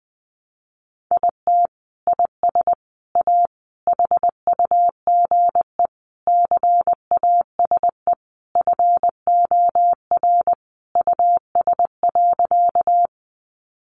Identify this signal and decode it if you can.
Morse code, heard as a keyed tone, about 20 wpm, 700 Hz. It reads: IT IS A HUGE CASE FOR US.
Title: Sentences from Top 1100 Words 20wpm